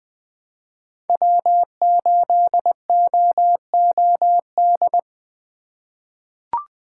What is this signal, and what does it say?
Morse code, heard as a keyed tone, about 20 wpm, 700 Hz. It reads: W8OOD
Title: USA Call Signs - 5 Characters - 4x Character Spacing - 20wpm